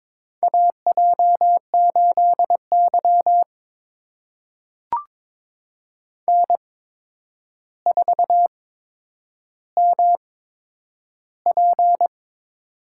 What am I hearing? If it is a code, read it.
Morse code, heard as a keyed tone, about 22 wpm, spaced as for 9 wpm, 700 Hz. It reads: AJ8Y N4MP
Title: USA Call Signs - 4 Characters - 8x Character Spacing - 22wpm